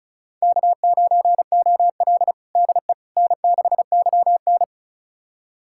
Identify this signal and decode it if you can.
Morse code, heard as a keyed tone, about 34 wpm, 700 Hz. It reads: K9OL DE N6YN
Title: USA Callsign DE Callsign - 4 Characters - 35wpm